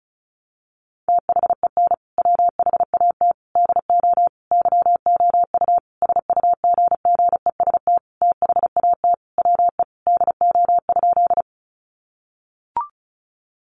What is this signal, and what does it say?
Morse code, heard as a keyed tone, about 34 wpm, 700 Hz. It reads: THEN WHAT DO YOU SUGGEST THAT WE DO?
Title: Sentences from Top 1300 Words 35wpm